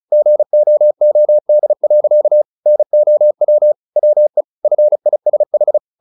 Morse code, heard as a keyed tone, about 34 wpm, 600 Hz. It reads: GOOD. NOW WE FISH
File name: the old man and the sea 2 35wpm